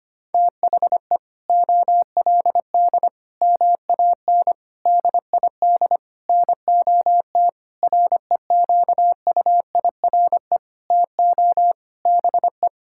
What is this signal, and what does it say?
Morse code, heard as a keyed tone, about 25 wpm, 700 Hz. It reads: THE OLD MAN DID NOT REQUIRE TO BE
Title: Sentences from Top 1600 Words 25wpm